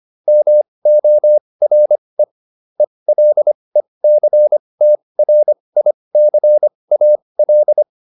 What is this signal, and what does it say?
Morse code, heard as a keyed tone, about 25 wpm, spaced as for 20 wpm, 600 Hz. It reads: MORE ELECTRICAL